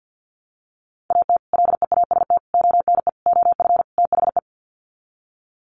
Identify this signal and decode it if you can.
Morse code, heard as a keyed tone, about 50 wpm, 700 Hz. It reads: AT LEAST ONE OF THE
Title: Sets of 5 Words Encore 50wpm